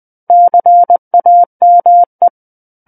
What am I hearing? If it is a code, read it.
Morse code, heard as a keyed tone, about 20 wpm, 700 Hz. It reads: CAME
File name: 20 WPM